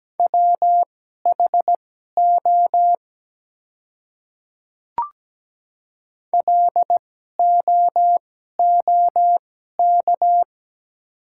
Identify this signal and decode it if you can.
Morse code, heard as a keyed tone, about 17 wpm, spaced as for 12 wpm, 690 Hz. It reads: WHO LOOK
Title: 2x Character Spacing - Top 300 Words - 17wpm